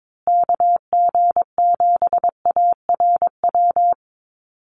Morse code, heard as a keyed tone, about 22 wpm, 700 Hz. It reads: KG7ARW
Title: USA Call Signs Repeated 3x 22wpm